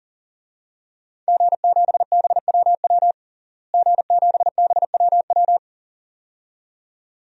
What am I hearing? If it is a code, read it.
Morse code, heard as a keyed tone, about 40 wpm, 700 Hz. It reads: G7BWW G7BWW
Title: DXCC-100 Call Signs - 5 Characters - Repeated 2x - 40wpm